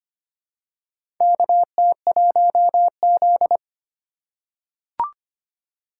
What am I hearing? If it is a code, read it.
Morse code, heard as a keyed tone, about 25 wpm, 700 Hz. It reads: KT1Z
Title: USA Call Signs - 4 Characters - 3x Character Spacing - 25wpm